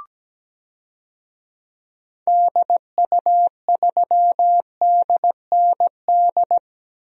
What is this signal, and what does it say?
Morse code, heard as a keyed tone, about 17 wpm, 700 Hz. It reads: DU3DND
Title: DXCC-100 Call Signs - 6 Characters - Repeated 2x - 17wpm